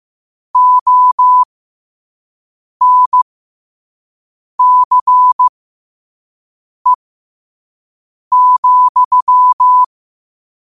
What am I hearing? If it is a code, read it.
Morse code, heard as a keyed tone, about 15 wpm, spaced as for 5 wpm, 1000 Hz. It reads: ONCE,
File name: qotdfiles / 5wpm